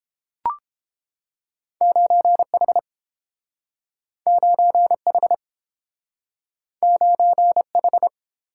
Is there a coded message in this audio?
Yes: Morse code from 1.8 to 8.1 s, heard as a keyed tone, about 30 wpm, 700 Hz. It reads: 9H 9H 9H